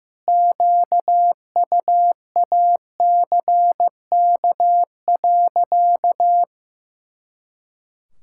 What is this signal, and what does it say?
Morse code, heard as a keyed tone, about 15 wpm, 700 Hz. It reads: QUACK.
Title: Pangram Challenge v2 - 15wpm